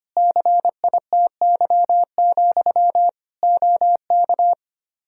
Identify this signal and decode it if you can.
Morse code, heard as a keyed tone, about 25 wpm, 700 Hz. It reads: CITY, OK
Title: Practice QSO - Exchange 1 - Beginning - 25wpm